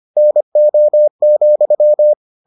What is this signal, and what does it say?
Morse code, heard as a keyed tone, about 25 wpm, 600 Hz. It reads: NO,